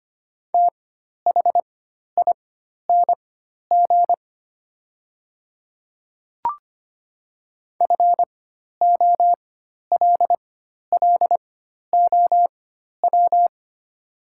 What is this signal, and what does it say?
Morse code, heard as a keyed tone, about 25 wpm, spaced as for 12 wpm, 700 Hz. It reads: THING FOLLOW
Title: 4x Character Spacing - Top 200 Words - 25wpm